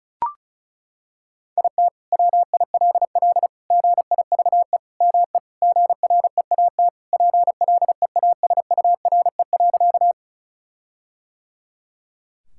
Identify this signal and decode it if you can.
Morse code, heard as a keyed tone, about 34 wpm, 700 Hz. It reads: IT WILL GIVE ME GREAT PLEASURE.
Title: Sentences from Top 1600 Words 35wpm